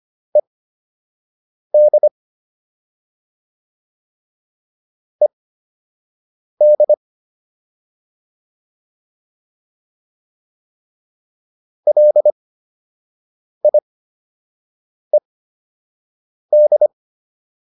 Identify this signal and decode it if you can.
Morse code, heard as a keyed tone, about 25 wpm, spaced as for 6 wpm, 600 Hz. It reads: ED ED LIED